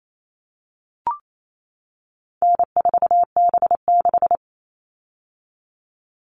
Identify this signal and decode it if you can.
Morse code, heard as a keyed tone, about 28 wpm, 700 Hz. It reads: N4B6
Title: All Cut Numbers - 4-Digit 28wpm